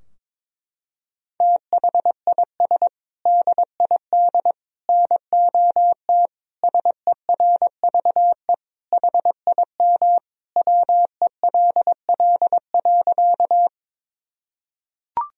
Morse code, heard as a keyed tone, about 22 wpm, 700 Hz. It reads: THIS DID NOT SERVE HIM WELL.
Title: Sentences from Top 1000 Words - Review Part 2 - 22wpm